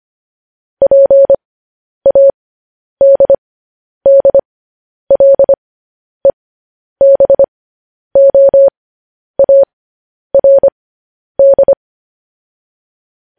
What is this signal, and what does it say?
Morse code, heard as a keyed tone, about 25 wpm, spaced as for 10 wpm, 550 Hz. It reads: PADDLEBOARD
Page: News Headlines in Morse Code at 10 WPM